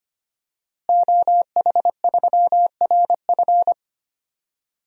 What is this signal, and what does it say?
Morse code, heard as a keyed tone, about 25 wpm, 700 Hz. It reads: OH3RF
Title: DXCC-100 Call Signs - 5 Characters - Repeated 2x - 25wpm